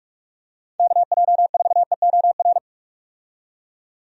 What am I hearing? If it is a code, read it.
Morse code, heard as a keyed tone, about 44 wpm, 700 Hz. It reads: KJ4EOR
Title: USA Call Signs - 6 Characters - 45wpm